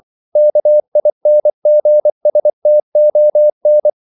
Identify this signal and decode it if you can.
Morse code, heard as a keyed tone, about 24 wpm, 600 Hz. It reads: KINGSTON
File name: CW / Practice / NationalCapitals24